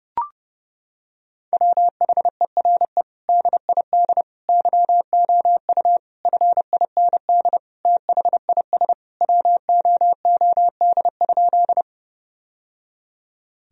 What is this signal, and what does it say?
Morse code, heard as a keyed tone, about 30 wpm, 700 Hz. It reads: WHERE DID YOU FIND THIS WOOD?
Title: Sentences from Top 1000 Words - Review Part 2 - 30wpm